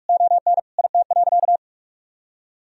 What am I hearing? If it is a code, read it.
Morse code, heard as a keyed tone, about 46 wpm, 700 Hz. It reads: ON IT.
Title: Sentences from Top 1000 Words - Review Part 4 - 45wpm